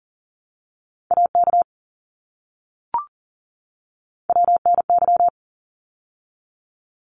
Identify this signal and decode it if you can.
Morse code, heard as a keyed tone, about 40 wpm, 700 Hz. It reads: AK WNY
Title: ARRL Sections 40wpm